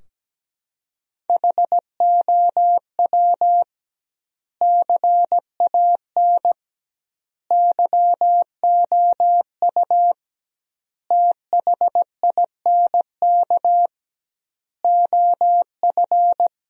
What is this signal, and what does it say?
Morse code, heard as a keyed tone, about 17 wpm, 700 Hz. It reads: HOW CAN YOU THINK OF
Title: 2x Word-Spacing - Sentences from Top 200 Words - 17wpm